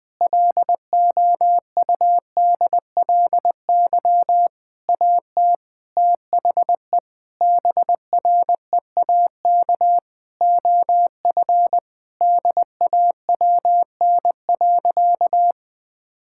Morse code, heard as a keyed tone, about 20 wpm, 700 Hz. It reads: LOUDLY AT THE BREAK OF DAWN.
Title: Difficult Sentences with Hints - 20wpm